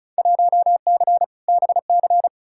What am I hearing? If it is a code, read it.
Morse code, heard as a keyed tone, about 36 wpm, 700 Hz. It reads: 1C BC
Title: Field Day Exchanges Encore - 35wpm